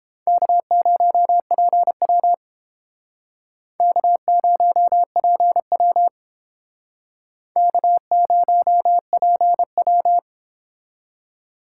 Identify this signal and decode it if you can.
Morse code, heard as a keyed tone, about 30 wpm, 700 Hz. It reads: K0PW K0PW K0PW